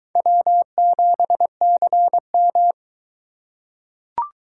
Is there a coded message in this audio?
Yes: Morse code from 0.2 to 2.7 s, heard as a keyed tone, about 22 wpm, 700 Hz. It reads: W7CM